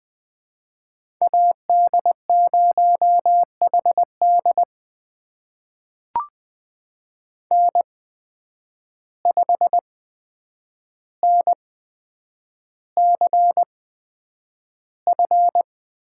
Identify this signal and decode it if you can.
Morse code, heard as a keyed tone, about 20 wpm, spaced as for 9 wpm, 700 Hz. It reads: AD0HD N5NCF
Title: USA Call Signs - 8x Character Spacing - 20wpm